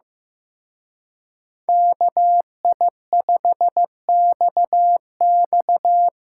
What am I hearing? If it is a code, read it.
Morse code, heard as a keyed tone, about 15 wpm, 700 Hz. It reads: KI5XX
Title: Speed Racing - USA 2x2 Call Signs 15wpm